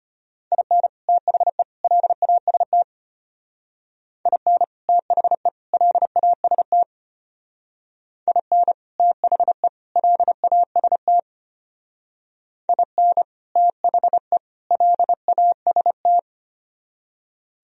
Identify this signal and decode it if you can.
Morse code, heard as a keyed tone, about 32 wpm, 700 Hz. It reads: IN THE LAST IN THE LAST IN THE LAST IN THE LAST